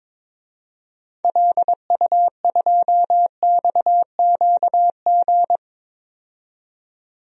Morse code, heard as a keyed tone, about 22 wpm, 700 Hz. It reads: LU2XQG
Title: DXCC-100 Call Signs - 6 Characters - 22wpm